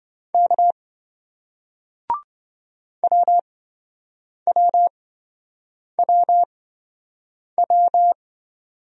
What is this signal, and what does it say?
Morse code, heard as a keyed tone, about 27 wpm, spaced as for 7 wpm, 700 Hz. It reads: K WWWW